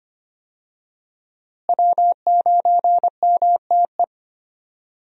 Morse code, heard as a keyed tone, about 25 wpm, 700 Hz. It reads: W9MTE
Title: USA Call Signs - 5 Characters - Repeated 2x - 25wpm